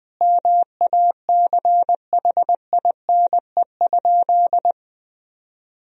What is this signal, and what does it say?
Morse code, heard as a keyed tone, about 21 wpm, 700 Hz. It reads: MACHINE?
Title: Sentences from Top 1000 Words - Review Part 2 - 20wpm